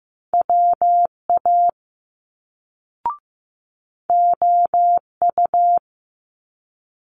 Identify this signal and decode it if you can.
Morse code, heard as a keyed tone, about 15 wpm, 700 Hz. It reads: WA OU
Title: Multi-letter Phonemes Quick Fire 15wpm